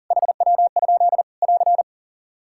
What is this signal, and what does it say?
Morse code, heard as a keyed tone, about 40 wpm, 700 Hz. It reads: HW? <AR>
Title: Practice QSO - Exchange 1 - Beginning - 40wpm